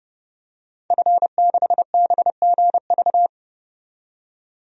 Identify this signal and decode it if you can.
Morse code, heard as a keyed tone, about 31 wpm, 700 Hz. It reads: F6BGV
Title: DXCC-100 Call Signs - 5 Characters - 30wpm